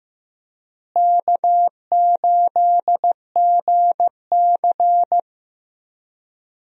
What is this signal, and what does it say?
Morse code, heard as a keyed tone, about 15 wpm, 700 Hz. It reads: K8GC
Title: USA Call Signs - 4 Characters - Repeated 3x - 15wpm